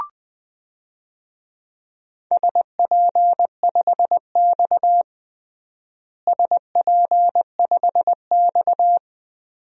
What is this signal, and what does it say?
Morse code, heard as a keyed tone, about 20 wpm, 700 Hz. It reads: SP5X SP5X
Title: DXCC-100 Call Signs - 4 Characters - Repeated 2x - 20wpm